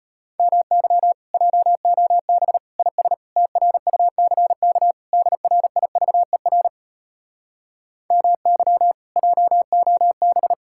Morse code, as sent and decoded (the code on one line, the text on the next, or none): -- -.-- / .--- --- -... / .. ... / - .-. ..- -.-. -.- / -.. .-. .. ...- . .-. / -- -.-- / .--- --- -...
MY JOB IS TRUCK DRIVER MY JOB